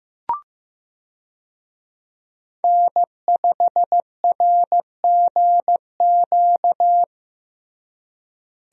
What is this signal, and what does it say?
Morse code, heard as a keyed tone, about 16 wpm, 700 Hz. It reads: N5RGQ
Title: USA Call Signs - 5 Characters - Repeated 3x - 15wpm